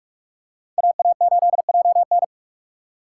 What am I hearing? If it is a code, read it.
Morse code, heard as a keyed tone, about 46 wpm, 700 Hz. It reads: AA8JN
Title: USA Call Signs - 5 Characters - 3x Character Spacing - 45wpm